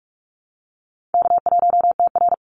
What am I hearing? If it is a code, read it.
Morse code, heard as a keyed tone, about 44 wpm, 700 Hz. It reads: K1TR
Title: Easy Call Signs - 45wpm